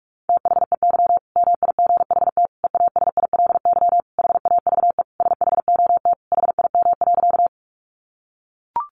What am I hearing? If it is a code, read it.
Morse code, heard as a keyed tone, about 44 wpm, 700 Hz. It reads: THEY MIGHT EASILY HAVE SHOT HIM.
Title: Sentences from Top 1100 Words 45wpm